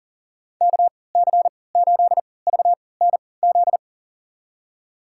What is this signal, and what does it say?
Morse code, heard as a keyed tone, about 39 wpm, spaced as for 22 wpm, 700 Hz. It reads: KC8VNZ
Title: USA Call Signs - 6 Characters - 3x Character Spacing - 40wpm